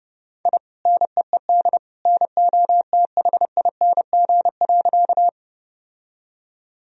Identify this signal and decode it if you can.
Morse code, heard as a keyed tone, about 32 wpm, 700 Hz. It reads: I NEED NOTHING.
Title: Sentences from Top 1000 Words - Review Part 1 - 30wpm